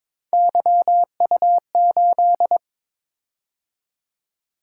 Morse code, heard as a keyed tone, about 22 wpm, 700 Hz. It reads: YU8
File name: practice / 22wpm / Three Letter-Number Combinations - 22wpm